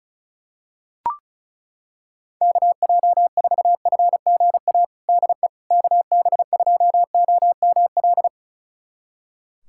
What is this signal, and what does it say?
Morse code, heard as a keyed tone, about 36 wpm, 700 Hz. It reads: KJ4FGA DE KB2OML
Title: USA Callsign DE Callsign - 6 Characters - 35wpm